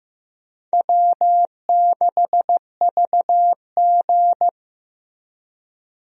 Morse code, heard as a keyed tone, about 15 wpm, 700 Hz. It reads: W6VG
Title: USA Call Signs - 4 Characters - Repeated 2x - 15wpm